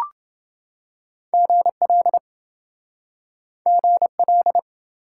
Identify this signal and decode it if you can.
Morse code, heard as a keyed tone, about 28 wpm, 700 Hz. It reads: GL GL